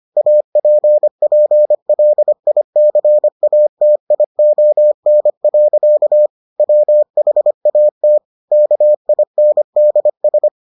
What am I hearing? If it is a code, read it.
Morse code, heard as a keyed tone, about 25 wpm, 600 Hz. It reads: APPLICATION. WHAT KINDS